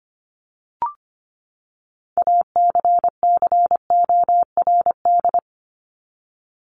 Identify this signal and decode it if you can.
Morse code, heard as a keyed tone, about 25 wpm, 700 Hz. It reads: ACCORD
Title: ING Suffix - Top 100 Words with Base Words 25wpm